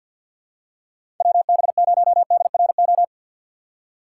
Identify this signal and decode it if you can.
Morse code, heard as a keyed tone, about 53 wpm, 700 Hz. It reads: WB0DRO
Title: USA Call Signs - 6 Characters - 4x Character Spacing - 50wpm